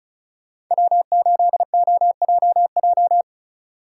Morse code, heard as a keyed tone, about 43 wpm, 700 Hz. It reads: W8OJJ
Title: Speed Racing - USA 1x3 Call Signs 35wpm